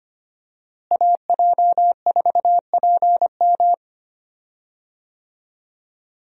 Morse code, heard as a keyed tone, about 25 wpm, 700 Hz. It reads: AJ4PM
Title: USA Call Signs - 5 Characters - Repeated 3x - 25wpm